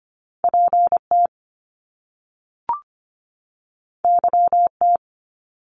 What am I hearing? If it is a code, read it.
Morse code, heard as a keyed tone, about 25 wpm, 700 Hz. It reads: PT YT
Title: Top Two Letter Combinations in Words 25wpm